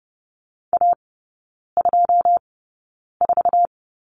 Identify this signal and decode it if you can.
Morse code, heard as a keyed tone, about 32 wpm, spaced as for 9 wpm, 700 Hz. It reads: A24
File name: 081 Lesson 14e - Number 2 - Three Characters 30wpm